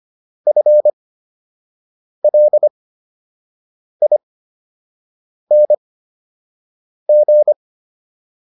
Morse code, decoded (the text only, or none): FLING